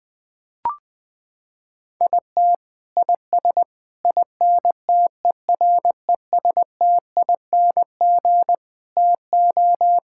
Morse code, heard as a keyed tone, about 20 wpm, 700 Hz. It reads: IT IS INTERESTING TO
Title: Sets of 6 Words - 20wpm